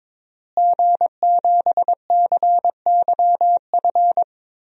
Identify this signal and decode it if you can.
Morse code, heard as a keyed tone, about 21 wpm, 700 Hz. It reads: G7CYF
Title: DXCC-100 Call Signs - 5 Characters - Repeated 2x - 22wpm